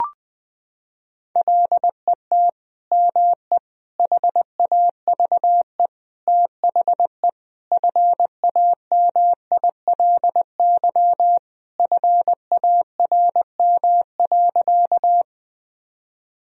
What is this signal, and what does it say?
Morse code, heard as a keyed tone, about 21 wpm, 700 Hz. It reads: LET ME HAVE THE FAMILY FARM.
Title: Sentences from Top 1000 Words - Review Part 2 - 20wpm